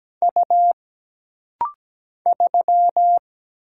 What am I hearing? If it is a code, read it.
Morse code, heard as a keyed tone, about 17 wpm, 700 Hz. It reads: U 3
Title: Single Letter-Number - Rapid-Fire 17wpm